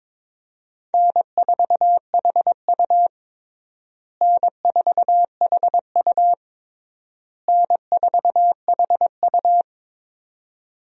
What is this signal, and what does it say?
Morse code, heard as a keyed tone, about 22 wpm, 700 Hz. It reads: N4HU N4HU N4HU
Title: USA Call Signs - 4 Characters - Repeated 3x - 22wpm